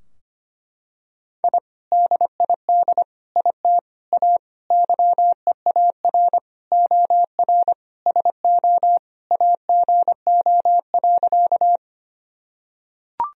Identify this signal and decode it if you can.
Morse code, heard as a keyed tone, about 25 wpm, 700 Hz. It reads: I DID IT A YEAR OR SO AGO.
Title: Sentences from Top 1000 Words - Review Part 3 - 25wpm